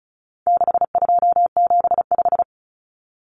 Morse code, heard as a keyed tone, about 34 wpm, 700 Hz. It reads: 6275
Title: Numbers - 4-Digit 35wpm